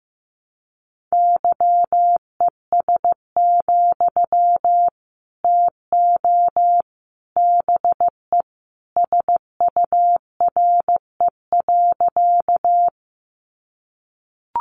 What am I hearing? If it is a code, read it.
Morse code, heard as a keyed tone, about 14 wpm, 700 Hz. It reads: YES, TO BE SURE.
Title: Sentences from Top 500 Words 15wpm